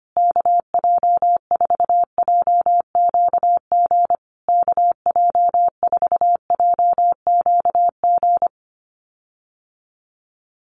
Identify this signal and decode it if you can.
Morse code, heard as a keyed tone, about 25 wpm, 700 Hz. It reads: KJ4JQG KJ4JQG
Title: USA Call Signs Repeated 3x 25wpm